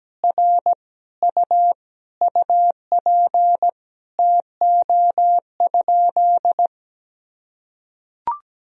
Morse code, heard as a keyed tone, about 17 wpm, 700 Hz. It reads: R U UP TO?
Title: Sentences from Top 100 Words and Abbreviations 17wpm